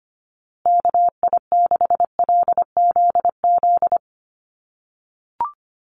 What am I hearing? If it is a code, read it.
Morse code, heard as a keyed tone, about 25 wpm, 700 Hz. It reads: KI6LZZ
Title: USA Call Signs Repeated 3x 25wpm